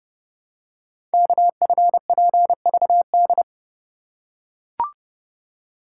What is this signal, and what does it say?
Morse code, heard as a keyed tone, about 30 wpm, 700 Hz. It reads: KFPVD
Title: Code Groups - 5 Letters - 30wpm